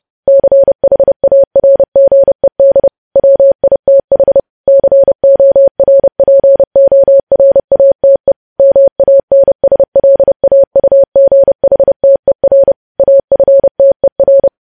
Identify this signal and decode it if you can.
Morse code, heard as a keyed tone, about 30 wpm, 550 Hz. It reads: CHARGED WITH CORPORATE MANSLAUGHTER AFTER
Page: news headlines in morse code at 30 wpm